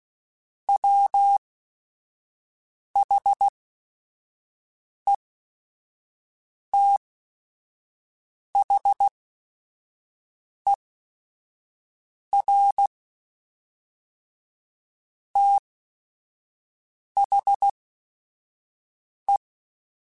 Morse code, heard as a keyed tone, about 16 wpm, spaced as for 5 wpm, 800 Hz. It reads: WHETHER THE
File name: Chuckles / Chuckles-mp3s / Glass Half Full @5